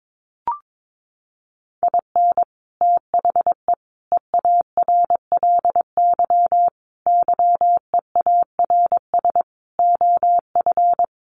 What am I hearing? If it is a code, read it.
Morse code, heard as a keyed tone, about 22 wpm, 700 Hz. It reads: IN THE EARLY YEARS OF
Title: Sets of 5 Words Encore 22wpm